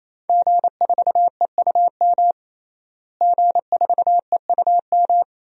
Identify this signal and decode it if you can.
Morse code, heard as a keyed tone, about 28 wpm, 700 Hz. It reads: G4EUM G4EUM
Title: DXCC-100 Call Signs - 5 Characters - Repeated 3x - 28wpm